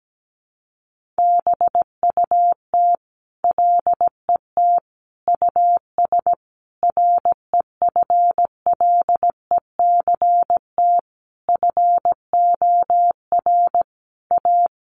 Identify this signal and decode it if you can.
Morse code, heard as a keyed tone, about 17 wpm, 700 Hz. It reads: BUT LET US REFLECT FOR A
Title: Sentences from Top 1500 Words 17wpm